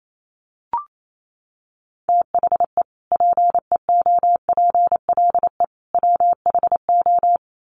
Morse code, heard as a keyed tone, about 28 wpm, 700 Hz. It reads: THE PEOPLE WHO
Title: Sets of 3 Words 28wpm